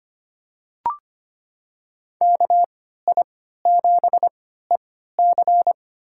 Morse code, heard as a keyed tone, about 25 wpm, spaced as for 14 wpm, 700 Hz. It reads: KI7EC
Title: USA Call Signs - 5 Characters - 3x Character Spacing - 25wpm